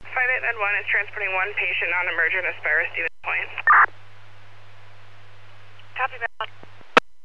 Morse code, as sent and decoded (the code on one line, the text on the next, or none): none